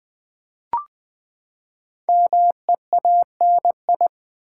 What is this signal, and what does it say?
Morse code, heard as a keyed tone, about 20 wpm, 700 Hz. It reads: MEANI
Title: Build A Word - 5 Letters At A Time - 10-Letter Words - 20wpm